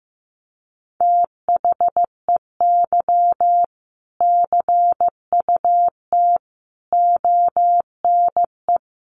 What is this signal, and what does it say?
Morse code, heard as a keyed tone, about 14 wpm, 700 Hz. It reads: THEY CUT ONE
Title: Sentences from Top 1400 Words 15wpm